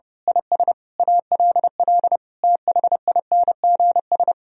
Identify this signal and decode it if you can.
Morse code, heard as a keyed tone, about 28 wpm, 700 Hz. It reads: IS ALL THINGS
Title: Speed-Racing - Sets of 5 Words 30wpm